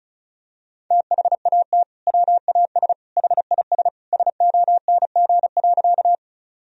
Morse code, heard as a keyed tone, about 34 wpm, 700 Hz. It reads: THAT WAS HIS SONG.
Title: Sentences from Top 1000 Words - Review Part 2 - 35wpm